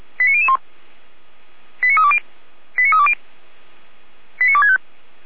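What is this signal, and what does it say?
Five-tone selective call (ZVEI), heard as quick runs of five tones, tones of about 70 ms. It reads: 89001 89320 89320 89265
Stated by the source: Five-tone ZVEI selective call received on an SDR, from audio.f85360.feuerwehr-orlamuende.org (Feuerwehr Orlamünde fire service).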